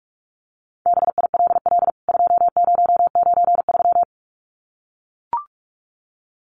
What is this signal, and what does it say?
Morse code, heard as a keyed tone, about 44 wpm, 700 Hz. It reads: BILL 2093
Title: CWT Exchange 45wpm